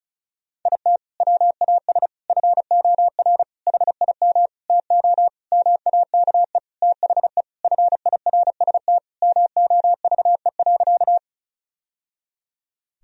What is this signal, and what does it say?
Morse code, heard as a keyed tone, about 36 wpm, 700 Hz. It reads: IT WAS FOR HIM TO MAKE THE FIRST MOVE.
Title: Sentences from Top 1000 Words - Review Part 1 - 35wpm